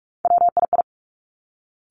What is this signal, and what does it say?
Morse code, heard as a keyed tone, about 46 wpm, 700 Hz. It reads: WII
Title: Three Challenging Letter-Number Combinations 45wpm